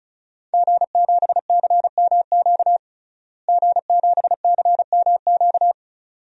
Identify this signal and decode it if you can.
Morse code, heard as a keyed tone, about 35 wpm, 700 Hz. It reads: G7CMQ G7CMQ
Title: DXCC-100 Call Signs - 5 Characters - Repeated 2x - 35wpm